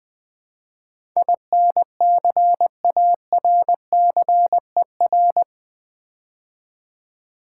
Morse code, heard as a keyed tone, about 20 wpm, 700 Hz. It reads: INCARCER
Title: Build A Word - 4 Letters At A Time - 11-Letter Words - 20wpm